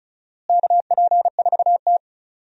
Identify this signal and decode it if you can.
Morse code, heard as a keyed tone, about 36 wpm, 700 Hz. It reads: KP4T
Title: DXCC-100 Call Signs - 4 Characters - Repeated 3x - 35wpm